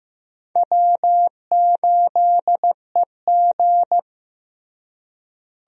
Morse code, heard as a keyed tone, about 15 wpm, 700 Hz. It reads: W8EG
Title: USA Call Signs - 4 Characters - Repeated 3x - 15wpm